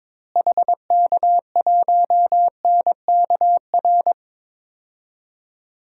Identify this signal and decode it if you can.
Morse code, heard as a keyed tone, about 22 wpm, 700 Hz. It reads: HK1NKR
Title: DXCC-100 Call Signs - 6 Characters - Repeated 2x - 22wpm